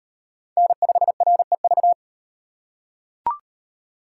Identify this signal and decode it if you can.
Morse code, heard as a keyed tone, about 35 wpm, 700 Hz. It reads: N5REV